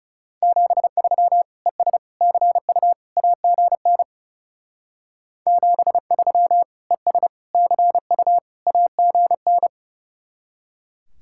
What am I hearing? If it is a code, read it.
Morse code, heard as a keyed tone, about 37 wpm, 700 Hz. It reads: 73 ES CU AGN 73 ES CU AGN
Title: Speed-Racing Common QSO Phrases Encore - 30wpm